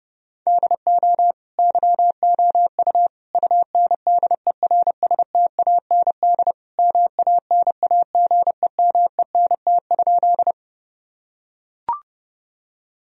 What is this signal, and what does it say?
Morse code, heard as a keyed tone, about 32 wpm, 700 Hz. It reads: DO YOU UNDERSTAND MANAGEMENT?
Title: Sentences from Top 1000 Words - Review Part 4 - 30wpm